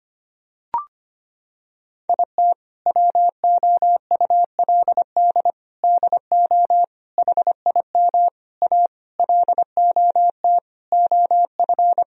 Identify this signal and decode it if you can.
Morse code, heard as a keyed tone, about 25 wpm, 700 Hz. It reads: IT WOULD DO HIM A LOT OF
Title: Sentences from Top 1000 Words - Review Part 3 - 25wpm